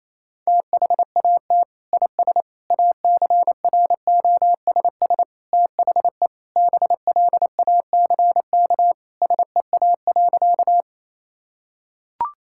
Morse code, heard as a keyed tone, about 28 wpm, 700 Hz. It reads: THAT IS ACROSS THE BLACK SEA.
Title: Sentences from Top 1000 Words - Review Part 5 - 28wpm